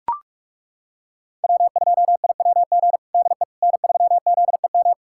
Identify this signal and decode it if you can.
Morse code, heard as a keyed tone, about 41 wpm, 700 Hz. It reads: W2IWG DE N3ZEM
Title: USA Callsign DE Callsign - 5 Characters - 45wpm